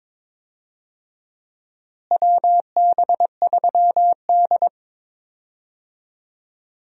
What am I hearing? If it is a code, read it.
Morse code, heard as a keyed tone, about 22 wpm, 700 Hz. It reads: WB3D
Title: USA Call Signs - 4 Characters - Repeated 3x - 22wpm